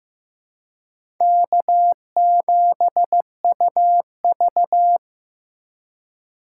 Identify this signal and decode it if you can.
Morse code, heard as a keyed tone, about 15 wpm, 700 Hz. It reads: K7UV
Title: USA Call Signs - 4 Characters - 3x Character Spacing - 15wpm